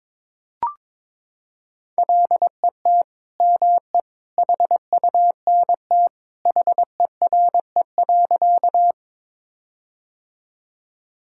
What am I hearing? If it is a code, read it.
Morse code, heard as a keyed tone, about 21 wpm, 700 Hz. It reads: LET ME HUNT HERE.
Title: Sentences from Top 1600 Words 22wpm